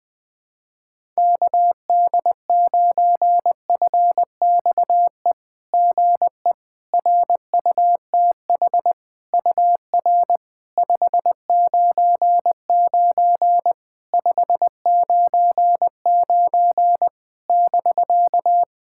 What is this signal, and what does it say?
Morse code, heard as a keyed tone, about 20 wpm, 660 Hz. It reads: KD9FXE GE RUTH UR 599 599 <BK>
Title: SOTA QSO Practice 20wpm